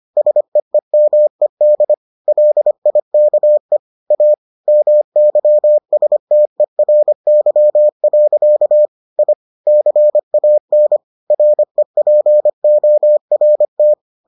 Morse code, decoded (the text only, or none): SEEMED LIKE A MYSTERY. I CAN REPORT